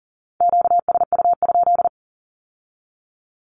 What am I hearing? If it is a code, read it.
Morse code, heard as a keyed tone, about 38 wpm, 700 Hz. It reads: QSU?
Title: Q-Codes with Spoken Meaning - 40wpm